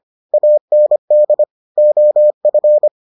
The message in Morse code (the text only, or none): AND OF